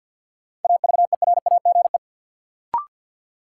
Morse code, heard as a keyed tone, about 49 wpm, 700 Hz. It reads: AVERAGE
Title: Sets of 5 Words - New Word Drill - 50wpm